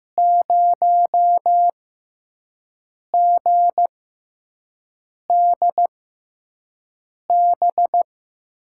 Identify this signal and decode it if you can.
Morse code, heard as a keyed tone, about 15 wpm, spaced as for 5 wpm, 700 Hz. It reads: 0GDB
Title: USA Call Signs - 6 Characters - 6x Character Spacing - 15wpm